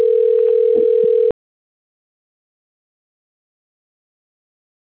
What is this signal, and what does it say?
Ringback tone (North America), 440 + 480 Hz.